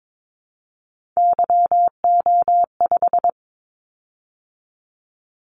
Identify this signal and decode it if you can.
Morse code, heard as a keyed tone, about 22 wpm, 700 Hz. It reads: YO5
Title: Callsign Prefixes for Top 25 DXCC - 22wpm